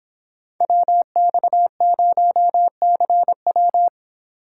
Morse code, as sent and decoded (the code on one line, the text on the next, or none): .-- -..- ----- -.-. .--
WX0CW